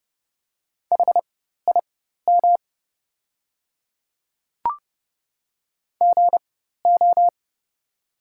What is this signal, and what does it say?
Morse code, heard as a keyed tone, about 30 wpm, spaced as for 14 wpm, 700 Hz. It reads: HIM GO
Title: 4x Character Spacing - Top 100 Words - 30wpm